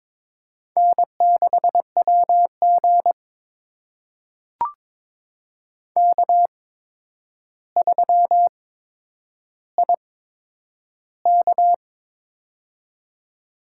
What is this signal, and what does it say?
Morse code, heard as a keyed tone, about 22 wpm, spaced as for 9 wpm, 700 Hz. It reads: N6WG K3IK
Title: USA Call Signs - 4 Characters - 8x Character Spacing - 22wpm